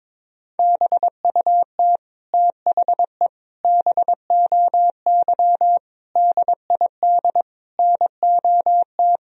Morse code, heard as a keyed tone, about 22 wpm, 700 Hz. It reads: BUT THE BOY DID NOT
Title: Sentences from Top 1000 Words - Review Part 3 - 22wpm